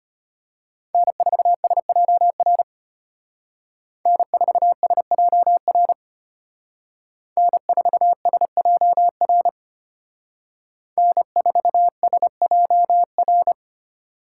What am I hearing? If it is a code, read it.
Morse code, heard as a keyed tone, about 32 wpm, 700 Hz. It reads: N4SJR N4SJR N4SJR N4SJR